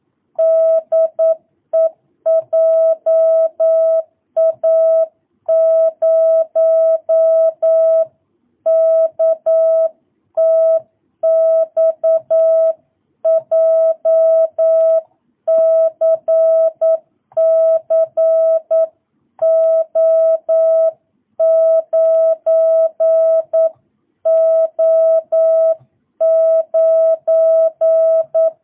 Morse code, heard as a keyed tone, about 9 wpm, 650 Hz. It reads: DE JA0KTX JCCO9O9